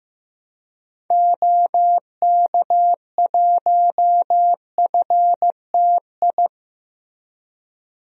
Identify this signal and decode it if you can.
Morse code, heard as a keyed tone, about 15 wpm, 700 Hz. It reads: OK1FTI
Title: DXCC-100 Call Signs - 6 Characters - Repeated 2x - 15wpm